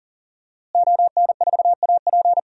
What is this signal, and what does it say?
Morse code, heard as a keyed tone, about 40 wpm, 700 Hz. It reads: ON4AP
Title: DXCC-100 Call Signs - 5 Characters - Repeated 2x - 40wpm